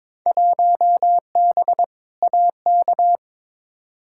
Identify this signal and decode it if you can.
Morse code, heard as a keyed tone, about 22 wpm, 700 Hz. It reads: 1B AK
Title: Field Day Exchanges Encore - 22wpm